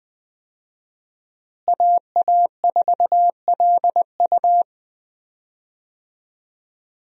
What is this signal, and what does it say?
Morse code, heard as a keyed tone, about 20 wpm, 700 Hz. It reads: AA4LU
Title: USA Call Signs - 5 Characters - Repeated 3x - 20wpm